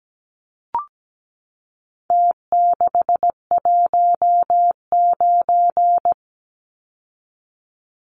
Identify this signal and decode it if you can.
Morse code, heard as a keyed tone, about 17 wpm, 700 Hz. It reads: T619
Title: Numbers - Contest Style - 4-Digit 17wpm